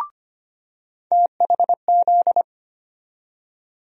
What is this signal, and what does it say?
Morse code, heard as a keyed tone, about 25 wpm, 700 Hz. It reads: THZ
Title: USA Call Sign Postfixes - 3 Letters 25wpm